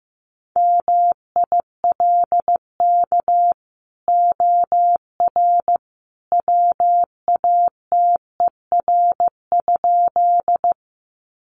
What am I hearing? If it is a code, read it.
Morse code, heard as a keyed tone, about 15 wpm, 700 Hz. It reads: MILK OR WATER?
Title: Sentences from Top 1500 Words 15wpm